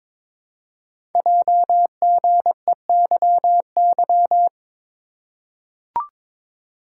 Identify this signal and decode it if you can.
Morse code, heard as a keyed tone, about 22 wpm, 700 Hz. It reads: JGEYY
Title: Code Groups - 5 Letters - 22wpm